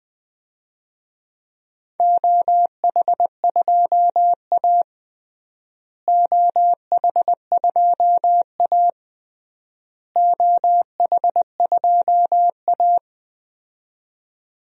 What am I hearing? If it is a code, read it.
Morse code, heard as a keyed tone, about 21 wpm, 700 Hz. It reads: OH2A OH2A OH2A
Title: DXCC-100 Call Signs - 4 Characters - Repeated 3x - 20wpm